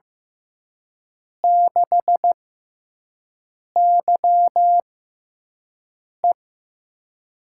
Morse code, heard as a keyed tone, about 15 wpm, spaced as for 5 wpm, 700 Hz. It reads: 6YE
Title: USA Call Signs - 5 Characters - 6x Character Spacing - 15wpm